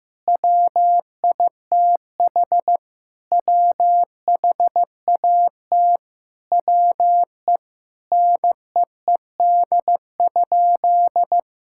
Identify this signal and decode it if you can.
Morse code, heard as a keyed tone, about 16 wpm, 700 Hz. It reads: WITH WHAT WE NEED?
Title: Sentences from Top 1000 Words - Review Part 5 - 15wpm